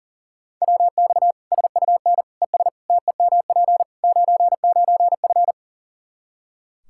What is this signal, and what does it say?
Morse code, heard as a keyed tone, about 40 wpm, 700 Hz. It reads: WX SUN ES TEMP 99F
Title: Practice QSO - Exchange 2 - Info - 40wpm